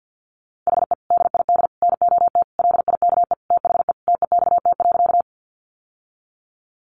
Morse code, heard as a keyed tone, about 50 wpm, 700 Hz. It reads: HE DID NOT LIKE THE TEXT.
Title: Sentences from Top 1200 Words 50wpm